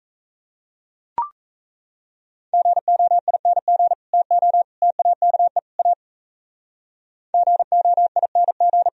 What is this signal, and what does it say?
Morse code, heard as a keyed tone, about 35 wpm, 700 Hz. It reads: GOING TO TAKE A GOING
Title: Speed-Racing - Sets of 4 Words 28wpm